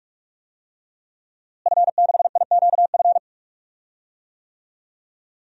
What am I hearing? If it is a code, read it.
Morse code, heard as a keyed tone, about 44 wpm, 700 Hz. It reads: F6IQF
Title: DXCC-100 Call Signs - 5 Characters - Repeated 2x - 45wpm